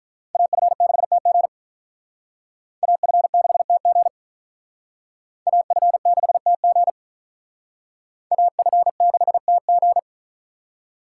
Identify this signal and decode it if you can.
Morse code, heard as a keyed tone, about 44 wpm, 700 Hz. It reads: AF6TG AF6TG AF6TG AF6TG